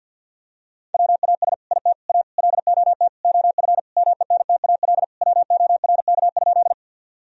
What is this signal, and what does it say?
Morse code, heard as a keyed tone, about 50 wpm, 700 Hz. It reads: WAS IT A LOT OF MENTAL WORK?
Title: Sentences from Top 1000 Words - Review Part 4 - 50wpm